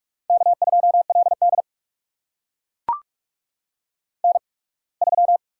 Morse code, heard as a keyed tone, about 44 wpm, spaced as for 27 wpm, 690 Hz. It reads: K2LD N3
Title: USA Call Signs - 4 Characters - 8x Character Spacing - 45wpm